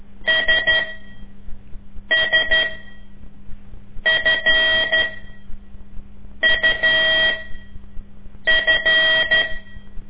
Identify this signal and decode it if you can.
Morse code, heard as a keyed tone, about 13 wpm, spaced as for 5 wpm, 1900 Hz. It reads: SSFUF
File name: ik1weg2 / CORSI CW / lez5-2